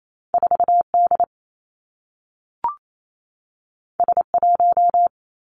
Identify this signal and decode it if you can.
Morse code, heard as a keyed tone, about 28 wpm, 700 Hz. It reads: 4 D S 1
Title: Two Challenging Letter-Number Combinations 28wpm